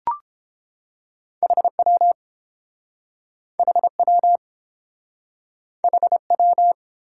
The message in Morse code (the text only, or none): HW HW HW